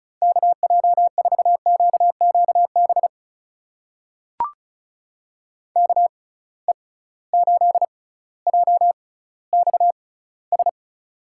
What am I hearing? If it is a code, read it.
Morse code, heard as a keyed tone, about 36 wpm, spaced as for 18 wpm, 700 Hz. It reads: KJ4QQB KE8JXS
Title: USA Call Signs - 6x Character Spacing - 35wpm